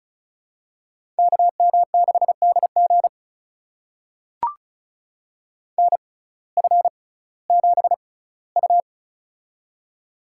Morse code, heard as a keyed tone, about 36 wpm, spaced as for 19 wpm, 700 Hz. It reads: KM6DG NF7U
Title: USA Call Signs - 6x Character Spacing - 35wpm